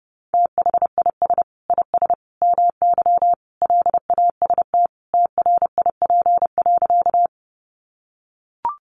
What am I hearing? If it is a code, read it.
Morse code, heard as a keyed tone, about 30 wpm, 700 Hz. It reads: THIS IS MY LAST TRIP.
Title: Sentences from Top 1300 Words 30wpm